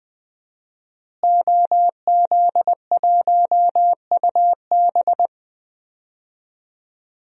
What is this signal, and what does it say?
Morse code, heard as a keyed tone, about 20 wpm, 700 Hz. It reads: OZ1UB
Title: DXCC-100 Call Signs - 5 Characters - Repeated 2x - 20wpm